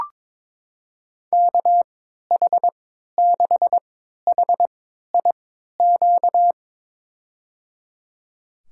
Morse code, heard as a keyed tone, about 21 wpm, spaced as for 12 wpm, 700 Hz. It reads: KH6HIQ
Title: USA Call Signs - 3x Character Spacing - 22wpm